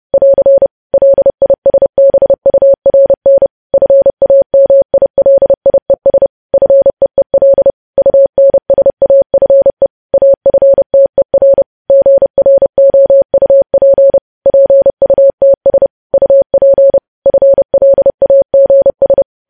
Morse code, heard as a keyed tone, about 30 wpm, 550 Hz. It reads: <AR> LISBURN FAMILIES FEEL UNSAFE AFTER GROUP PUTS UP FLAGS
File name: podcast / News headlines-1x-30x30